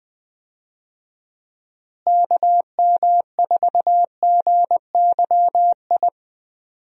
Morse code, heard as a keyed tone, about 20 wpm, 700 Hz. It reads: KM4GYI
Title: USA Call Signs - 6 Characters - Repeated 2x - 20wpm